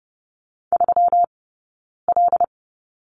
Morse code, heard as a keyed tone, about 30 wpm, 700 Hz. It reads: 3 L
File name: Lesson 20d - Number 3 - Two Characters 30wpm